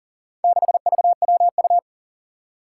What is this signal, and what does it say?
Morse code, heard as a keyed tone, about 40 wpm, 700 Hz. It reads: 6VWU
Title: Four Challenging Letter-Number Combinations 40wpm